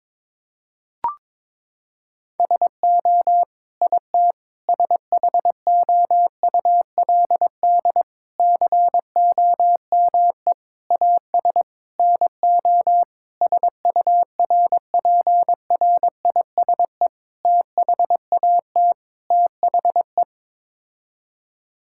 Morse code, heard as a keyed tone, about 22 wpm, 700 Hz. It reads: SO IT SHOULD COME AS NO SURPRISE THAT THE
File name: Sets of 9 Words - 22wpm